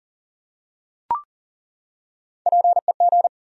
Morse code, heard as a keyed tone, about 40 wpm, 700 Hz. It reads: PEG
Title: USA Common Names - Max 3 Letters - 40wpm